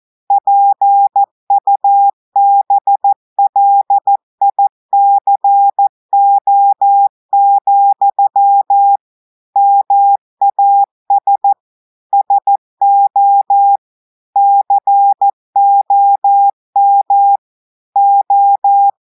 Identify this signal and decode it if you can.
Morse code, heard as a keyed tone, about 14 wpm, 800 Hz. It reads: PUBLICO, MAS SO COM O
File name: audios / portugais / portugais texte en clair 2 vitesse 14 mots minutes